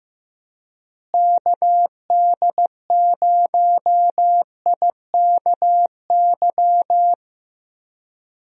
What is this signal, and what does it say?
Morse code, heard as a keyed tone, about 15 wpm, 700 Hz. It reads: KD0IKY
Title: USA Call Signs - 6 Characters - Repeated 3x - 15wpm